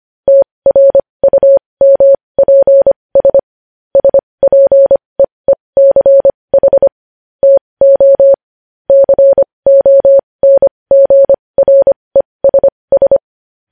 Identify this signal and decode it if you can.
Morse code, heard as a keyed tone, about 25 wpm, spaced as for 20 wpm, 550 Hz. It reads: TRUMPS SPEECH TO CONGRESS
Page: news-headlines-in-morse-code-at-20-wpm